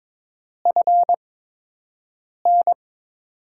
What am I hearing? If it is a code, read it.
Morse code, heard as a keyed tone, about 23 wpm, spaced as for 6 wpm, 700 Hz. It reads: FN
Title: USA Call Signs - 6 Characters - 8x Character Spacing - 22wpm